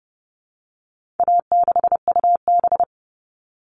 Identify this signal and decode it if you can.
Morse code, heard as a keyed tone, about 30 wpm, 700 Hz. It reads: A6UB
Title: All Cut Numbers - 4-Digit 30wpm